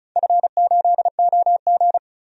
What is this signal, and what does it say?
Morse code, heard as a keyed tone, about 36 wpm, 700 Hz. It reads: F8OG
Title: DXCC-100 Call Signs - 4 Characters - 35wpm